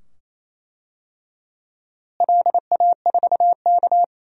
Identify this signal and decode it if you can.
Morse code, heard as a keyed tone, about 27 wpm, 700 Hz. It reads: LA4K
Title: DXCC-100 Call Signs - 4 Characters - Repeated 2x - 28wpm